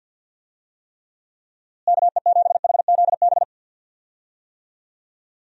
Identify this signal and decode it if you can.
Morse code, heard as a keyed tone, about 50 wpm, 700 Hz. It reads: KE7HZB
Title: USA Call Signs - 6 Characters - Repeated 2x - 50wpm